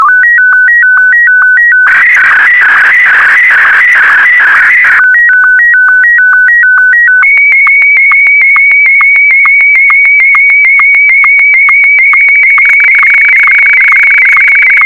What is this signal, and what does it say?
SSTV (slow-scan television), Martin 1.
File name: sstv / M1-VK4MSL